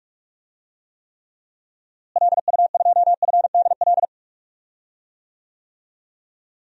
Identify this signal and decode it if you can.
Morse code, heard as a keyed tone, about 46 wpm, 700 Hz. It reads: LU2FDL
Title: DXCC-100 Call Signs - 6 Characters - Repeated 3x - 45wpm